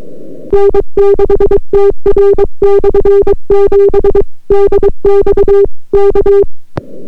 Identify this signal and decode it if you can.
Morse code, heard as a keyed tone, about 22 wpm, 400 Hz. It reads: N6TR/7 DX K